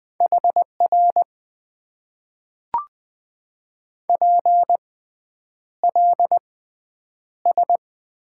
Morse code, heard as a keyed tone, about 20 wpm, spaced as for 9 wpm, 700 Hz. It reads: HR PLS
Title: Top 90 QSO Elements - 6x Character Spacing - 20wpm